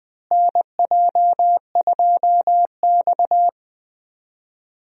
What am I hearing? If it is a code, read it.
Morse code, heard as a keyed tone, about 21 wpm, 700 Hz. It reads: NJ2X
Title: USA Call Signs - 4 Characters - Repeated 3x - 20wpm